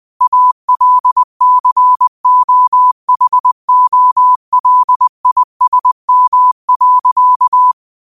Morse code, heard as a keyed tone, about 20 wpm, 1000 Hz. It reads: ALCOHOLISM.
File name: qotdfiles / 20wpm